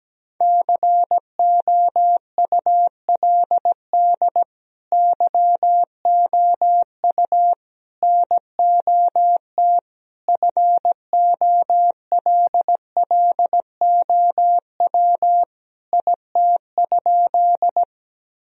Morse code, heard as a keyed tone, about 17 wpm, 700 Hz. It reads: COULD YOU NOT FOLLOW IT?
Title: Sentences from Top 1000 Words - Review Part 1 - 17wpm